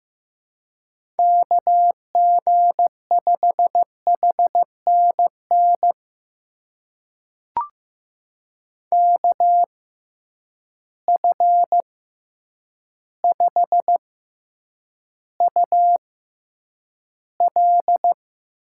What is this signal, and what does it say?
Morse code, heard as a keyed tone, about 15 wpm, spaced as for 8 wpm, 700 Hz. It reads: KG5HNN KF5UL
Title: USA Call Signs - 6x Character Spacing - 15wpm